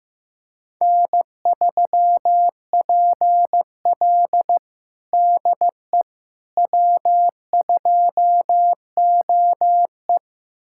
Morse code, heard as a keyed tone, about 15 wpm, 700 Hz. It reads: N3PL DE W2OE
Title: USA Callsign DE Callsign - 4 Characters - 15wpm